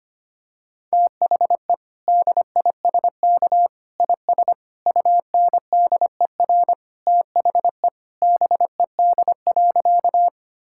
Morse code, heard as a keyed tone, about 25 wpm, 700 Hz. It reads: THE DISK IS UNDER THE BED.
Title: Sentences from Top 1000 Words - Review Part 6 - 25wpm